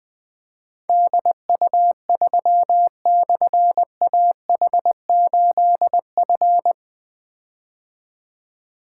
Morse code, heard as a keyed tone, about 20 wpm, 700 Hz. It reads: DU3/AH8F
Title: Difficult Call Signs 20wpm